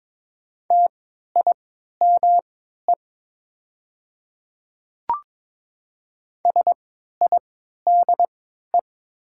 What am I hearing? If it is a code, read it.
Morse code, heard as a keyed tone, about 21 wpm, spaced as for 12 wpm, 700 Hz. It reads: TIME SIDE
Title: 3x Character Spacing - Top 500 Words - 22wpm